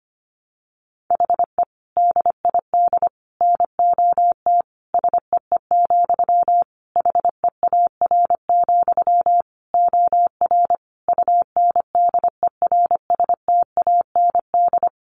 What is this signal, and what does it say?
Morse code, heard as a keyed tone, about 25 wpm, 700 Hz. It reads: HE DID NOT SEE, HEAR, OR UNDERSTAND
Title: Sentences from Top 1300 Words 25wpm